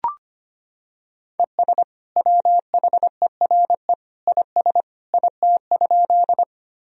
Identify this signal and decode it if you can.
Morse code, heard as a keyed tone, about 25 wpm, 700 Hz. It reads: ES WHERE IS IT?
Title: Sentences from Top 200 Words and Abbreviations 25wpm